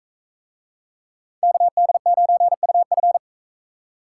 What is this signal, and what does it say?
Morse code, heard as a keyed tone, about 40 wpm, 700 Hz. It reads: KD9UF